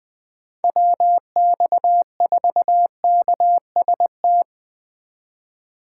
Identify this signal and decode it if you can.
Morse code, heard as a keyed tone, about 20 wpm, 700 Hz. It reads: WX4KST
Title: Speed Racing - USA Call Signs 17wpm